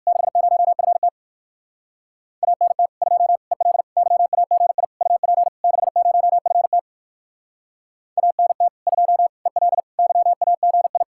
Morse code, heard as a keyed tone, about 64 wpm, 700 Hz. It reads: 60FT ANT 2 EL YAGI UP 60FT ANT 2 EL YAGI